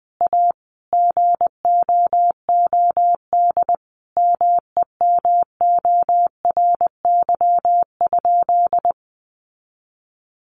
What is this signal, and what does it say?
Morse code, heard as a keyed tone, about 20 wpm, 700 Hz. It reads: A GOOD MEMORY?
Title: Sentences from Top 1300 Words 20wpm